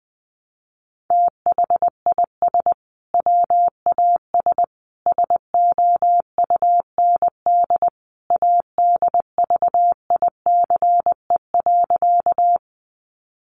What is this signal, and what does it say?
Morse code, heard as a keyed tone, about 20 wpm, 700 Hz. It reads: THIS WAS SOUND ADVICE.
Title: Sentences from Top 1100 Words 20wpm